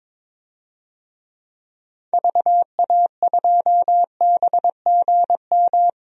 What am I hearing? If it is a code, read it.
Morse code, heard as a keyed tone, about 22 wpm, 700 Hz. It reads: VA2BGM
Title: DXCC-100 Call Signs - 6 Characters - Repeated 2x - 22wpm